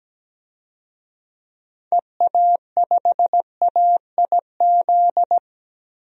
Morse code, heard as a keyed tone, about 17 wpm, 700 Hz. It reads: EA5AIZ
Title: DXCC-100 Call Signs - 6 Characters - 17wpm